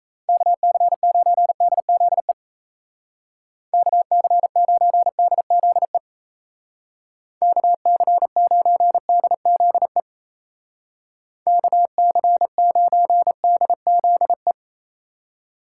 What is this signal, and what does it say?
Morse code, heard as a keyed tone, about 33 wpm, 700 Hz. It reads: KC9DZE KC9DZE KC9DZE KC9DZE
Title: Speed Racing - USA Call Signs 28wpm